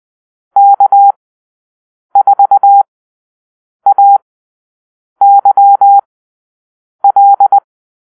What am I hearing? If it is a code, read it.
Morse code, heard as a keyed tone, about 20 wpm, spaced as for 7 wpm, 800 Hz. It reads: K4AYL
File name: arrl / Call-Sign Drill-9